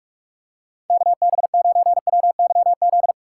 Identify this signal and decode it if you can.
Morse code, heard as a keyed tone, about 46 wpm, 690 Hz. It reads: KB9WYZ
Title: USA Call Signs - 6 Characters - 3x Character Spacing - 45wpm